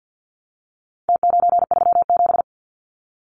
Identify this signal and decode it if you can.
Morse code, heard as a keyed tone, about 48 wpm, 700 Hz. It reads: T937
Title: Numbers - Contest Style - 4-Digit 50wpm